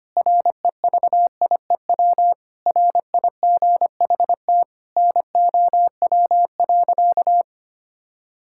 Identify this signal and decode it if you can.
Morse code, heard as a keyed tone, about 25 wpm, 700 Hz. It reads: REVIEW RIGHT NOW.
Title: Sentences from Top 1000 Words - Review Part 6 - 25wpm